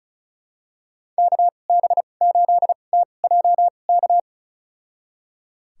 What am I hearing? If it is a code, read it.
Morse code, heard as a keyed tone, about 36 wpm, spaced as for 25 wpm, 700 Hz. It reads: KB8TJK
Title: USA Call Signs - 2x Character Spacing - 35wpm